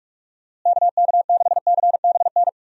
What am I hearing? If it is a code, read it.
Morse code, heard as a keyed tone, about 44 wpm, 700 Hz. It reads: KK6CBN
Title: USA Call Signs - 6 Characters - Repeated 2x - 45wpm